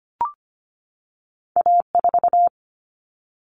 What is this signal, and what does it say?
Morse code, heard as a keyed tone, about 25 wpm, 700 Hz. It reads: A4
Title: Common Cut Numbers - 2-Digit 25wpm